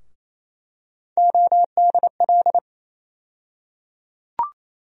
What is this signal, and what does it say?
Morse code, heard as a keyed tone, about 27 wpm, 700 Hz. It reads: ODL
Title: USA Call Sign Postfixes - 3 Letters 28wpm